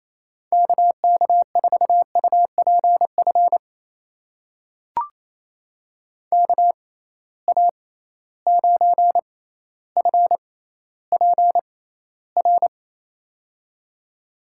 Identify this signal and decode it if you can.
Morse code, heard as a keyed tone, about 28 wpm, spaced as for 14 wpm, 700 Hz. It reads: KK4UPF KA9FPR
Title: USA Call Signs - 6 Characters - 6x Character Spacing - 28wpm